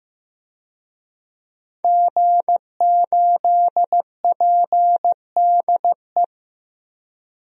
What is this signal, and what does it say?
Morse code, heard as a keyed tone, about 15 wpm, 700 Hz. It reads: G8PDE
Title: DXCC-100 Call Signs - 5 Characters - 15wpm